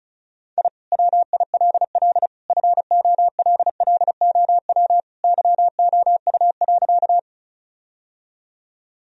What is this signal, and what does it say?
Morse code, heard as a keyed tone, about 36 wpm, 700 Hz. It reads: I WILL FOLLOW YOU.
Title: Sentences from Top 1000 Words - Review Part 1 - 35wpm